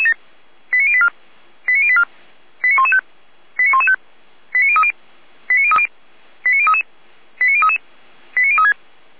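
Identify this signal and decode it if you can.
Five-tone selective call (ZVEI), heard as quick runs of five tones, tones of about 70 ms. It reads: 89084 89084 89115 89115 89030 89030 89033 89033 89036
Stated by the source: ZVEI five-tone selective call received on SDR.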